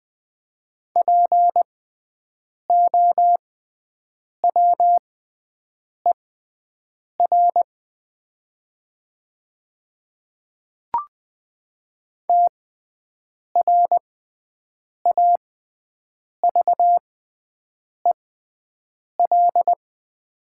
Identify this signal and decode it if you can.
Morse code, heard as a keyed tone, about 20 wpm, spaced as for 7 wpm, 700 Hz. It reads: POWER TRAVEL
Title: 6x Character Spacing - Top 500 Words - 20wpm